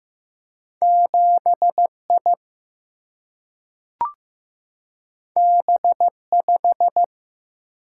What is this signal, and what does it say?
Morse code, heard as a keyed tone, about 15 wpm, 700 Hz. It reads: 7 I B 5
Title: Single-Letter Single-Number 15wpm